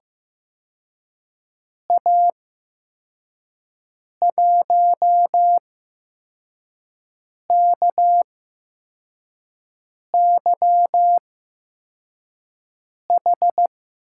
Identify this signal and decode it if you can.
Morse code, heard as a keyed tone, about 15 wpm, 700 Hz. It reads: A 1 K Y H
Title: USA Call Signs - 8x Character Spacing - 15wpm